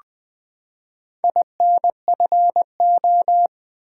Morse code, heard as a keyed tone, about 20 wpm, 700 Hz. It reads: INFO
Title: Top 180 QSO Elements 20wpm